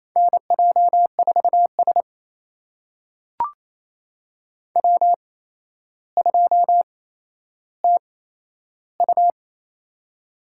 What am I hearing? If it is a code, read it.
Morse code, heard as a keyed tone, about 28 wpm, spaced as for 12 wpm, 700 Hz. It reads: NJ4S W2TU
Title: USA Call Signs - 8x Character Spacing - 28wpm